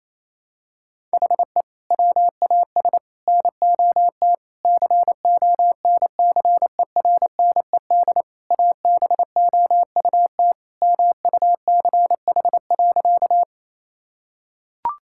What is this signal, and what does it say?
Morse code, heard as a keyed tone, about 28 wpm, 700 Hz. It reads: HE WAS NOT CONCERNED ABOUT MUCH.
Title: Sentences from Top 1000 Words - Review Part 6 - 28wpm